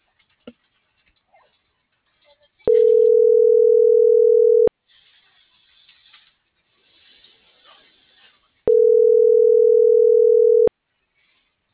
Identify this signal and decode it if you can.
Ringback tone (North America), 440 + 480 Hz.